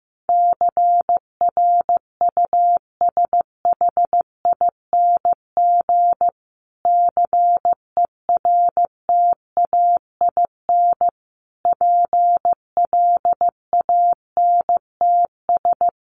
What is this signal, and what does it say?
Morse code, heard as a keyed tone, about 15 wpm, 700 Hz. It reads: CRUSHING CERTAIN PLANTS
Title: A Sound Of Thunder 15wpm